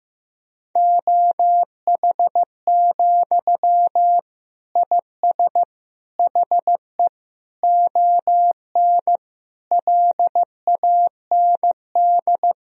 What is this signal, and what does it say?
Morse code, heard as a keyed tone, about 15 wpm, 700 Hz. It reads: OH, IS HE ON LAND
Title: Sentences from Top 1000 Words - Review Part 1 - 15wpm